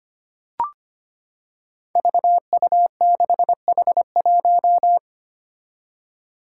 Morse code, heard as a keyed tone, about 25 wpm, 700 Hz. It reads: VU6H1
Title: Five Challenging Letter-Number Combinations 25wpm